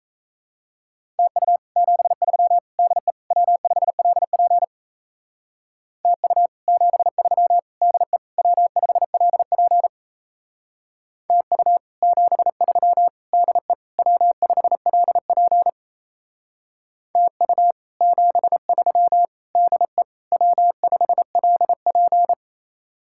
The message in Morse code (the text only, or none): TU 73 DE W5LP TU 73 DE W5LP TU 73 DE W5LP TU 73 DE W5LP